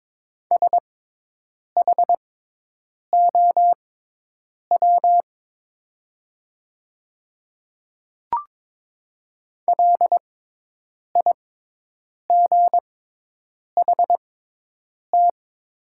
Morse code, heard as a keyed tone, about 22 wpm, spaced as for 8 wpm, 700 Hz. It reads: SHOW LIGHT
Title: 6x Character Spacing - Top 300 Words - 22wpm